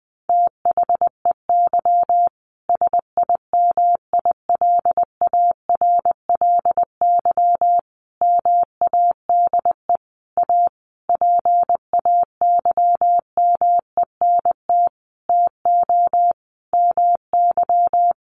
Morse code, heard as a keyed tone, about 20 wpm, 700 Hz. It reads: THEY SIMILARLY MADE A PAYMENT TO MY
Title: Sentences from Top 1400 Words 20wpm